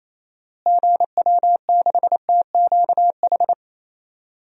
Morse code, heard as a keyed tone, about 27 wpm, 700 Hz. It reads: GW6TQH
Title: DXCC-100 Call Signs - 6 Characters - Repeated 2x - 28wpm